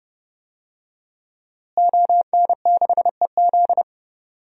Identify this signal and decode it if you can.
Morse code, heard as a keyed tone, about 30 wpm, 700 Hz. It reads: ON6EZ
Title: DXCC-100 Call Signs - 5 Characters - Repeated 3x - 30wpm